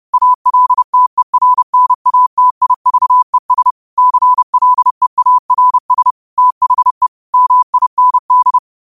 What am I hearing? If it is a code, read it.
Morse code, heard as a keyed tone, about 30 wpm, 1000 Hz. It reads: ALTERNATIVES CLEARS THE MIND